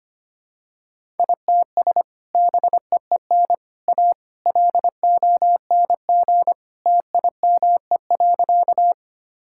Morse code, heard as a keyed tone, about 25 wpm, 700 Hz. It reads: ITS BEEN A LONG TIME.
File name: practice / 25wpm / Sentences with 3-Word Phrases - 25wpm